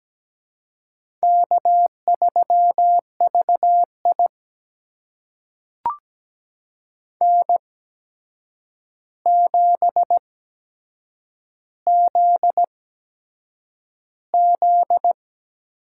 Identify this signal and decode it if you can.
Morse code, heard as a keyed tone, about 17 wpm, spaced as for 7 wpm, 700 Hz. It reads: K3VI N7ZZ
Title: USA Call Signs - 4 Characters - 8x Character Spacing - 17wpm